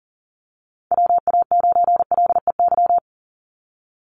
Morse code, heard as a keyed tone, about 40 wpm, 700 Hz. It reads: WA9LEY
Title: Call Signs Repeated 2x - 40wpm